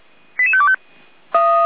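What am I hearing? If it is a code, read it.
Five-tone selective call (ZVEI), heard as quick runs of five tones, tones of about 70 ms. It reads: 88525
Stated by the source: Received on an SDR by feuerwehr-orlamuende (fire brigade Orlamünde); ZVEI five-tone call 88402.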